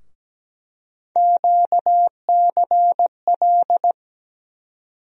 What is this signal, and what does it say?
Morse code, heard as a keyed tone, about 17 wpm, 700 Hz. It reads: QCL
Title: USA Call Sign Postfixes - 3 Letters 17wpm